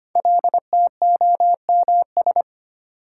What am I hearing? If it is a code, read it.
Morse code, heard as a keyed tone, about 25 wpm, 700 Hz. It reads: LTOMS
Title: Code Groups - 5 Letter:Numbers - 25wpm